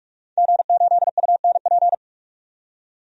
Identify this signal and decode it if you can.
Morse code, heard as a keyed tone, about 42 wpm, 700 Hz. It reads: G8UNP
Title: DXCC-100 Call Signs - 5 Characters - 45wpm